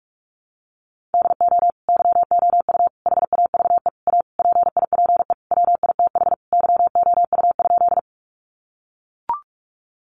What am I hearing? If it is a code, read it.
Morse code, heard as a keyed tone, about 44 wpm, 700 Hz. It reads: DO YOU HAVE A PIPE WITH YOU?
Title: Sentences from Top 1500 Words 45wpm